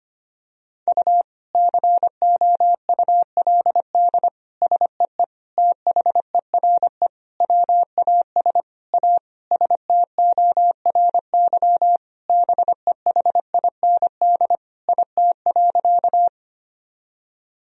Morse code, heard as a keyed tone, about 25 wpm, 700 Hz. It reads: U COULD SEE THERE WAS A STORY BEHIND IT.
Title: Sentences from Top 500 Words and Abbreviations 25wpm